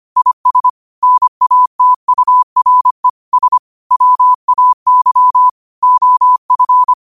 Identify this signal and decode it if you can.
Morse code, heard as a keyed tone, about 25 wpm, 1000 Hz. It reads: IS NATURE S WAY OF
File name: 25wpm